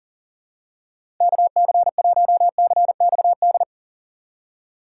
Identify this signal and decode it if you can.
Morse code, heard as a keyed tone, about 40 wpm, 700 Hz. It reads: KC1CXD
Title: USA Call Signs - 6 Characters - 3x Character Spacing - 40wpm